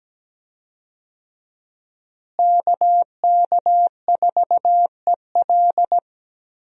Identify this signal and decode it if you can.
Morse code, heard as a keyed tone, about 17 wpm, 700 Hz. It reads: KK4EL
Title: USA Call Signs - 5 Characters - Repeated 3x - 17wpm